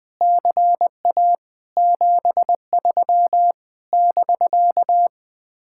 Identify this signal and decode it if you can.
Morse code, heard as a keyed tone, about 20 wpm, 700 Hz. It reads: CA 73 <BK>
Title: Speed-Racing Common QSO Phrases Encore - 20wpm